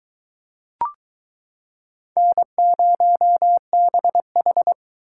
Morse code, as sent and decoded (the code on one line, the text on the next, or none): -. ----- -... ....
N0BH